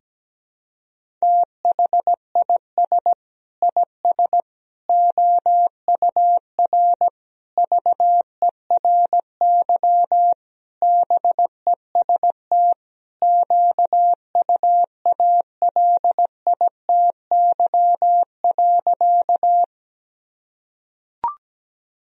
Morse code, heard as a keyed tone, about 17 wpm, 700 Hz. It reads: THIS IS OUR VERY BEST QUALITY.
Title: Sentences from Top 1000 Words - Review Part 4 - 17wpm